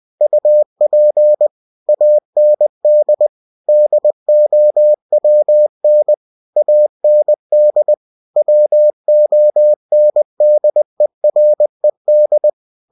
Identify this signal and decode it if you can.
Morse code, heard as a keyed tone, about 20 wpm, 600 Hz. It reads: UP AND DOWN AND WONDERED